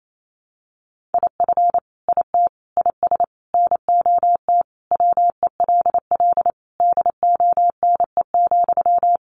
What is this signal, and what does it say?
Morse code, heard as a keyed tone, about 28 wpm, 700 Hz. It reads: IF IT IS NOT WELL DONE,
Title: Sentences from Top 1200 Words 28wpm